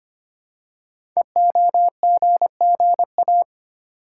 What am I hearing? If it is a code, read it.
Morse code, heard as a keyed tone, about 26 wpm, 700 Hz. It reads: EOGGA
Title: Code Groups - 5 Letters - 25wpm